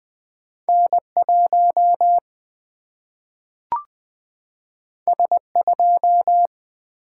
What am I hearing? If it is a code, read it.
Morse code, heard as a keyed tone, about 20 wpm, 700 Hz. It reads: N1 S2
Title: Single-Letter Single-Number 20wpm